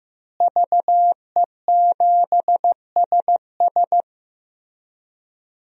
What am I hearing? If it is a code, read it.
Morse code, heard as a keyed tone, about 15 wpm, 700 Hz. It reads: VE7SS
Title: DXCC-100 Call Signs - 5 Characters - Repeated 3x - 15wpm